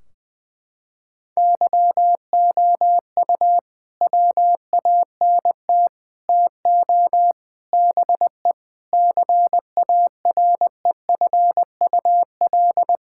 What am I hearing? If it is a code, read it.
Morse code, heard as a keyed tone, about 20 wpm, 700 Hz. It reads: YOU WANT TO BE CAREFUL
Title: Sentences from Top 1600 Words 20wpm